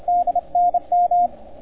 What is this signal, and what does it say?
Morse code, heard as a keyed tone, about 29 wpm, 710 Hz, with radio noise. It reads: D NM